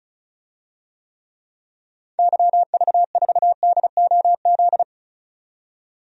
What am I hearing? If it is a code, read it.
Morse code, heard as a keyed tone, about 34 wpm, 700 Hz. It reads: YV4DOZ
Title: DXCC-100 Call Signs - 6 Characters - 35wpm